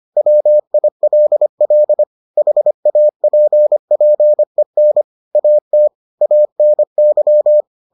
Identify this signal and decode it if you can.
Morse code, heard as a keyed tone, about 25 wpm, 600 Hz. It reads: WILL HAPPEN AT ANY